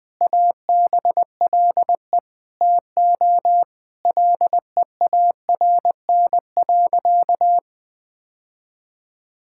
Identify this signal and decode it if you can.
Morse code, heard as a keyed tone, about 21 wpm, 700 Hz. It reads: ABLE TO LEARN.
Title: Sentences with 3-Word Phrases - 20wpm